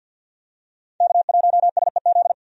Morse code, heard as a keyed tone, about 51 wpm, 700 Hz. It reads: K1SEZ
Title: USA Call Signs - 5 Characters - 50wpm